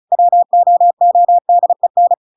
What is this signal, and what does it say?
Morse code, heard as a keyed tone, about 37 wpm, 700 Hz. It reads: WOODEN